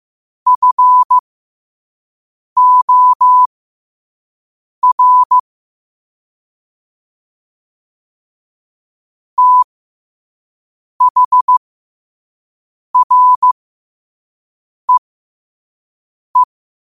Morse code, heard as a keyed tone, about 15 wpm, spaced as for 5 wpm, 1000 Hz. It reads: FOR THREE
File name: qotdfiles / 5wpm